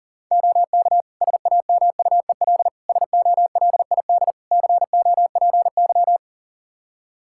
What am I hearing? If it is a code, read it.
Morse code, heard as a keyed tone, about 40 wpm, 700 Hz. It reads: OK SAMUEL SOLID COPY
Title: K1USN SST QSO Practice 40wpm